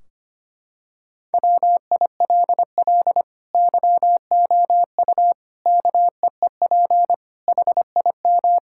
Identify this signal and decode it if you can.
Morse code, heard as a keyed tone, about 25 wpm, 700 Hz. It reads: WILL YOU KEEP HIM
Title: Sentences from Top 1600 Words 25wpm